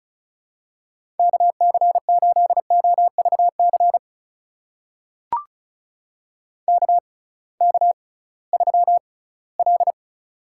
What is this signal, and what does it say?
Morse code, heard as a keyed tone, about 36 wpm, spaced as for 20 wpm, 700 Hz. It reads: KC8OVC KK3L
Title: USA Call Signs - 6x Character Spacing - 35wpm